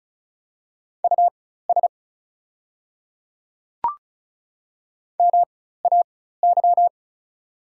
Morse code, heard as a keyed tone, about 36 wpm, spaced as for 16 wpm, 690 Hz. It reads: US MAY
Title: 4x Character Spacing - Top 200 Words - 35wpm